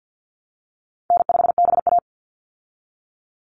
Morse code, heard as a keyed tone, about 50 wpm, 700 Hz. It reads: N5BA
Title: Easy Call Signs Repeated 3x - 50wpm